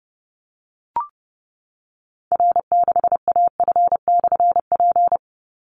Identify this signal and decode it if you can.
Morse code, heard as a keyed tone, about 30 wpm, 700 Hz. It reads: R6AF/P
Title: Call Signs Repeated 3x - 30wpm